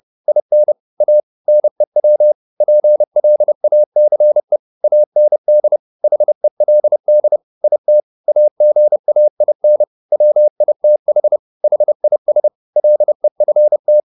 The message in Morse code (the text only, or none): IN A NEW PLACE AND HELD IT AGAIN WITH HIS LEFT